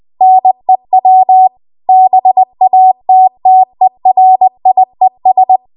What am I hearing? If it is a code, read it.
Morse code, heard as a keyed tone, about 20 wpm, 750 Hz. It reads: NEW BATTERIES
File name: Morse / 20WPM